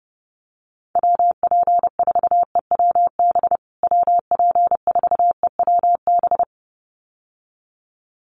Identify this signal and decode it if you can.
Morse code, heard as a keyed tone, about 30 wpm, 700 Hz. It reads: WP4EWB WP4EWB
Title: USA Call Signs Repeated 2x 30wpm